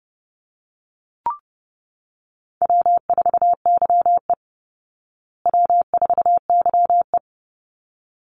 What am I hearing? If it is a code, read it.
Morse code, heard as a keyed tone, about 30 wpm, 700 Hz. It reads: W4YE W4YE
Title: Call Signs Repeated 3x - 30wpm